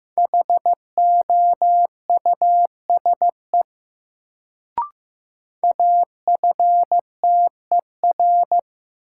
Morse code, heard as a keyed tone, about 15 wpm, 700 Hz. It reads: HOUSE AFTER
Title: Sets of 2 Words - New Word Drill - 15wpm